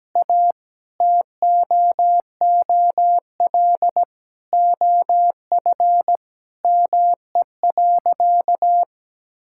Morse code, heard as a keyed tone, about 17 wpm, 700 Hz. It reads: A TOOL OF ME.
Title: Sentences from Top 1000 Words - Review Part 6 - 17wpm